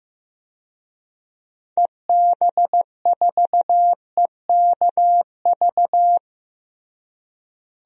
Morse code, heard as a keyed tone, about 15 wpm, 700 Hz. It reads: EB4EKV
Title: DXCC-100 Call Signs - 6 Characters - Repeated 3x - 15wpm